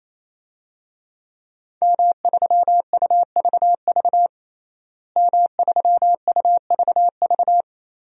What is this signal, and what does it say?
Morse code, heard as a keyed tone, about 27 wpm, 700 Hz. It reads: M3UVV M3UVV
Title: DXCC-100 Call Signs - 5 Characters - Repeated 3x - 28wpm